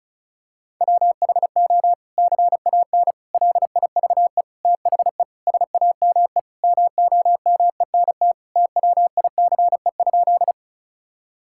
Morse code, heard as a keyed tone, about 34 wpm, 700 Hz. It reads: WHO CAN LIVE THE SAME MOMENT TWICE?
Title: Sentences from Top 1600 Words 35wpm